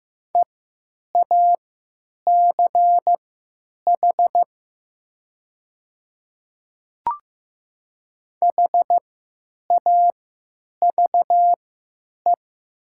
Morse code, heard as a keyed tone, about 15 wpm, spaced as for 9 wpm, 700 Hz. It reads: EACH HAVE
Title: 3x Character Spacing - Top 100 Words - 15wpm